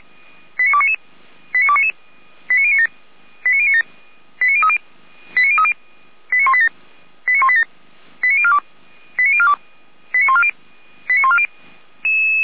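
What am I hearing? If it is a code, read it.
Five-tone selective call (ZVEI), heard as quick runs of five tones, tones of about 70 ms. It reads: 89299 89299 89097 89097 89030 89030 89187 89187 89042 89042 89140 89140
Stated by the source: ZVEI five-tone call received by SDR.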